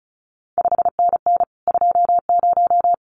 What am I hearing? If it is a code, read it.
Morse code, heard as a keyed tone, about 36 wpm, 700 Hz. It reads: 5NN 20
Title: CQ WW DX Contest Exchange Encore - 35wpm